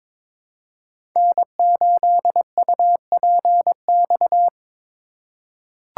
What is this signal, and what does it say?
Morse code, heard as a keyed tone, about 22 wpm, 700 Hz. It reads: N8UPX
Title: USA Call Signs - 4x Character Spacing - 22wpm